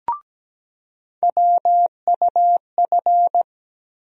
Morse code, heard as a keyed tone, about 17 wpm, 700 Hz. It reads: WUF
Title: USA Call Sign Postfixes - 3 Letters 17wpm